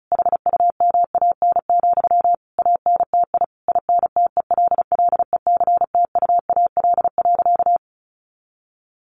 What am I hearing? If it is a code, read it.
Morse code, heard as a keyed tone, about 34 wpm, 700 Hz. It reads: HUMAN, ANTI INTELLECTUAL.
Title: A Sound Of Thunder 35wpm